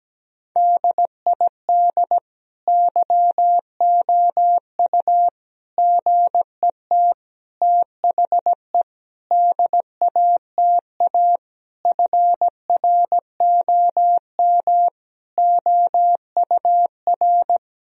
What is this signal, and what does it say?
Morse code, heard as a keyed tone, about 17 wpm, 700 Hz. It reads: DID YOU GET THE DATA FROM OUR
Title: Sentences from Top 1000 Words - Review Part 3 - 17wpm